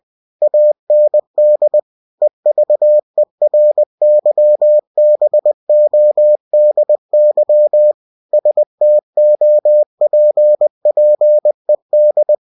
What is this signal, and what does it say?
Morse code, heard as a keyed tone, about 20 wpm, 600 Hz. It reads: AND EVERYBODY STOPPED
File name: cw-books / winnie-the-pooh / winnie-20wpm-page-0107